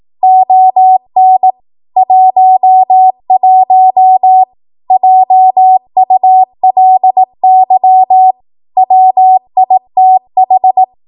Morse code, heard as a keyed tone, about 18 wpm, 750 Hz. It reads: ON 11 JULY WITH
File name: Morse / 18WPM